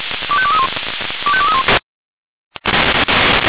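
Five-tone selective call (ZVEI), heard as quick runs of five tones, tones of about 70 ms. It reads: 25231 25231
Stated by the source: ZVEI five-tone selective call received by SDR.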